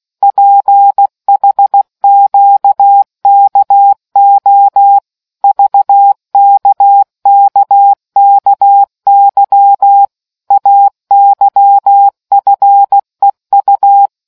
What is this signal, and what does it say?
Morse code, heard as a keyed tone, about 16 wpm, 800 Hz. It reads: PHQKO VKKKY AYFEU